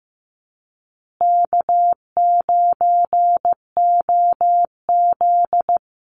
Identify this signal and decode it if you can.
Morse code, heard as a keyed tone, about 15 wpm, 700 Hz. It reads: K9OZ
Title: Easy Call Signs Repeated 3x - 15wpm